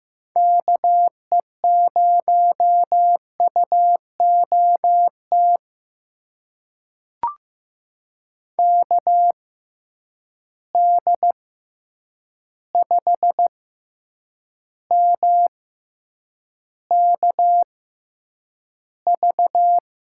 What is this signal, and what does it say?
Morse code, heard as a keyed tone, about 15 wpm, spaced as for 8 wpm, 700 Hz. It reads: KE0UOT KD5MKV
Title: USA Call Signs - 6 Characters - 6x Character Spacing - 15wpm